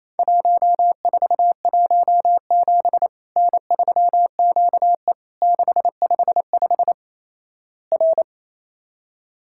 Morse code, heard as a keyed tone, about 28 wpm, 700 Hz. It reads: 1417 N3QE 655 R
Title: Worked All Europe DX Contest - QTC Exchange - 28wpm